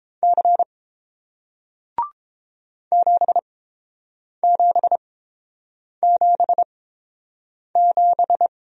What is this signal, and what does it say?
Morse code, heard as a keyed tone, about 29 wpm, spaced as for 7 wpm, 700 Hz. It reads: C 7777